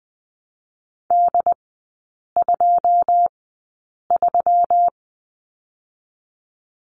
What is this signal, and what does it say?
Morse code, heard as a keyed tone, about 20 wpm, 700 Hz. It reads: D 2 3
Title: 121 Lesson 20e - Number 3 - Three Characters 20wpm